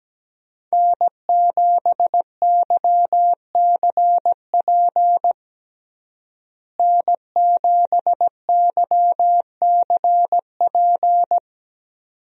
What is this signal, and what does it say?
Morse code, heard as a keyed tone, about 17 wpm, 700 Hz. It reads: N7YCP N7YCP
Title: USA Call Signs - 5 Characters - Repeated 2x - 17wpm